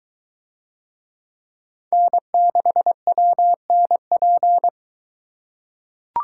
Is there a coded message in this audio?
Yes: Morse code from 1.9 to 4.7 s, heard as a keyed tone, about 22 wpm, 700 Hz. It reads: N6WNP